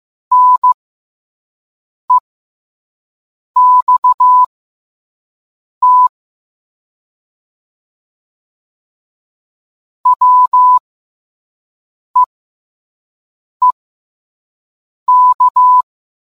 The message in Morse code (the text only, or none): NEXT WEEK